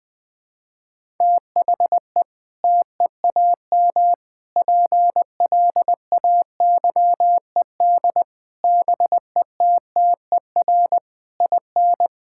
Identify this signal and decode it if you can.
Morse code, heard as a keyed tone, about 20 wpm, 700 Hz. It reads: THE TEAM PLAYED BETTER IN
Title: Sentences with 6-Word Phrases - Spoken Beginning - 20wpm